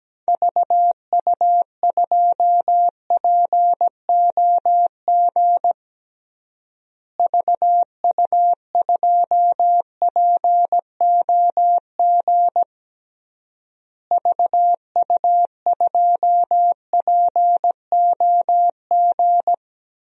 Morse code, heard as a keyed tone, about 17 wpm, 700 Hz. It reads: VU2POG VU2POG VU2POG
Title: DXCC-100 Call Signs - 6 Characters - Repeated 3x - 17wpm